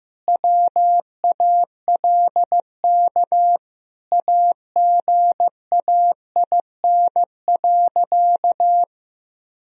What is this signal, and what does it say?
Morse code, heard as a keyed tone, about 15 wpm, 700 Hz. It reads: WALK AGAIN.
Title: Sentences from Top 1000 Words - Review Part 1 - 15wpm